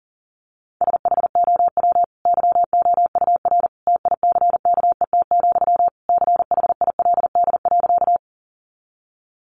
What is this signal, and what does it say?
Morse code, heard as a keyed tone, about 40 wpm, 700 Hz. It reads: SHOW YOUR TICKET, CHILD.
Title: Sentences from Top 1400 Words 40wpm